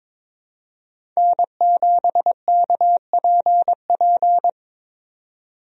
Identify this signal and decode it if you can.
Morse code, heard as a keyed tone, about 21 wpm, 700 Hz. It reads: N7KPP
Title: USA Call Signs - 5 Characters - 22wpm